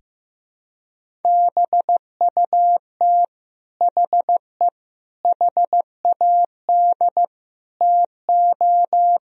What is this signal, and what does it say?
Morse code, heard as a keyed tone, about 15 wpm, 700 Hz. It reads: BUT HE HAD TO
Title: Sentences from Top 100 Words and Abbreviations 15wpm